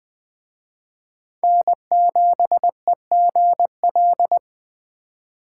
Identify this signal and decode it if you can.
Morse code, heard as a keyed tone, about 21 wpm, 700 Hz. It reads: N7EGL